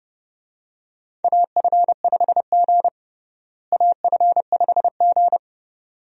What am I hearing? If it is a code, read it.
Morse code, heard as a keyed tone, about 30 wpm, 700 Hz. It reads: AF5G AF5G
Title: USA Call Signs - 4 Characters - Repeated 2x - 30wpm